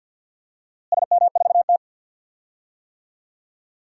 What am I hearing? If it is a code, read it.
Morse code, heard as a keyed tone, about 47 wpm, 700 Hz. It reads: SM4T
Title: DXCC-100 Call Signs - 4 Characters - Repeated 3x - 50wpm